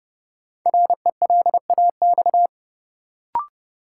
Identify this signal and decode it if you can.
Morse code, heard as a keyed tone, about 32 wpm, 700 Hz. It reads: RELAX
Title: Sentences from Top 1400 Words - New Word Drill Alt - 30wpm